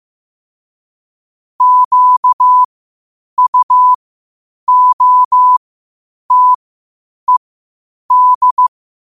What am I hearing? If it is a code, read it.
Morse code, heard as a keyed tone, about 15 wpm, spaced as for 8 wpm, 1000 Hz. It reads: QUOTED
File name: qotdfiles / 8wpm